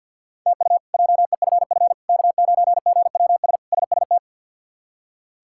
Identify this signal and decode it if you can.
Morse code, heard as a keyed tone, about 48 wpm, 700 Hz. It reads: TU JEFF K9GWS SST
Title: K1USN SST QSO Practice - 50wpm